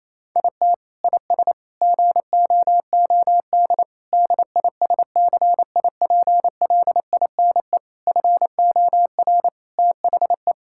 Morse code, heard as a keyed tone, about 28 wpm, 700 Hz. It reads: IT IS GOOD DISCIPLINE FOR THE
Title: Sentences from Top 1000 Words - Review Part 6 - 28wpm